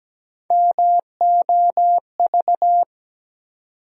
Morse code, heard as a keyed tone, about 17 wpm, 700 Hz. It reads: MOV
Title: USA Call Sign Postfixes - 3 Letters 17wpm